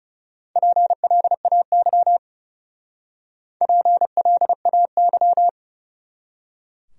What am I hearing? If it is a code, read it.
Morse code, heard as a keyed tone, about 36 wpm, 700 Hz. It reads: PLAY PLAY
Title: Speed-Racing Top 200 Words 30wpm